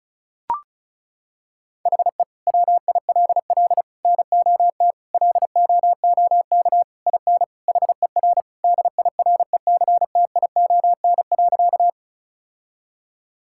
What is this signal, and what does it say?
Morse code, heard as a keyed tone, about 36 wpm, 700 Hz. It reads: HE WILL NOT LOOK IN HER DIRECTION.
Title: Sentences from Top 1000 Words - Review Part 5 - 35wpm